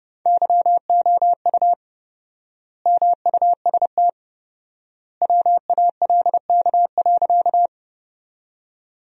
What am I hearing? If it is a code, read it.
Morse code, heard as a keyed tone, about 32 wpm, 700 Hz. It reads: YOU MUST WALK.
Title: 4x Word-Spacing - Sentences from Top 300 Words - 30wpm